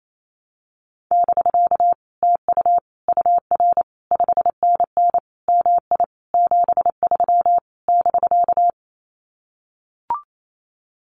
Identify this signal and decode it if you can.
Morse code, heard as a keyed tone, about 28 wpm, 700 Hz. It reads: <BK> TU UR 5NN MI 73 <BK>
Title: Common QSO Phrases Encore 28wpm